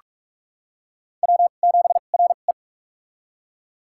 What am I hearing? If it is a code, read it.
Morse code, heard as a keyed tone, about 46 wpm, spaced as for 32 wpm, 700 Hz. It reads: W7RE
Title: USA Call Signs - 4 Characters - 2x Character Spacing - 45wpm